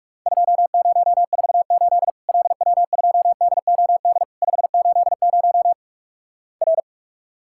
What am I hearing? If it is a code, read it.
Morse code, heard as a keyed tone, about 48 wpm, 700 Hz. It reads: 2048 LW2DOD 580 R
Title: Worked All Europe DX Contest - QTC Exchange - 45wpm